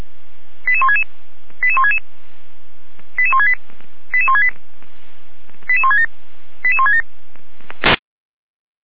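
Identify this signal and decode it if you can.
Five-tone selective call (ZVEI), heard as quick runs of five tones, tones of about 70 ms. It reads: 88166 88166 88168 88168 88167 88167
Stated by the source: ZVEI five-tone selective call received on an SDR.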